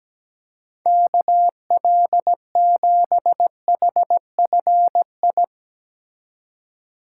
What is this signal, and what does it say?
Morse code, heard as a keyed tone, about 17 wpm, 700 Hz. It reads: KL7HFI
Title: K1USN SST QSO Practice - 17wpm